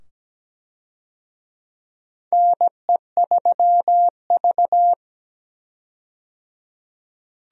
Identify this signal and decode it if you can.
Morse code, heard as a keyed tone, about 17 wpm, 700 Hz. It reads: NE3V
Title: USA Call Signs - 4 Characters - 17wpm